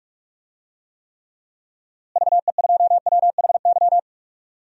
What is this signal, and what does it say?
Morse code, heard as a keyed tone, about 42 wpm, 700 Hz. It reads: VE2WHY
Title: DXCC-100 Call Signs - 6 Characters - Repeated 3x - 45wpm